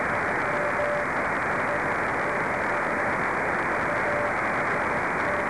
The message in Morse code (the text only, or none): none